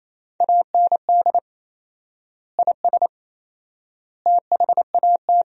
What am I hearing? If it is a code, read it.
Morse code, heard as a keyed tone, about 28 wpm, 700 Hz. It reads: AND IS THAT
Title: 4x Word-Spacing - Sentences from Top 100 Words - 28wpm